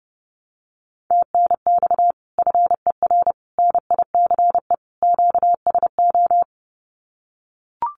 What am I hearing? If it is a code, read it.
Morse code, heard as a keyed tone, about 30 wpm, 700 Hz. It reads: TNX FER NICE QSO
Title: Common QSO Phrases 30wpm